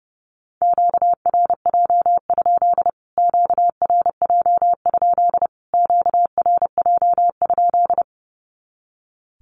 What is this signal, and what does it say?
Morse code, heard as a keyed tone, about 30 wpm, 700 Hz. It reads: QRJ? QRJ? QRJ?
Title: Q-Codes Repeated 3x 30wpm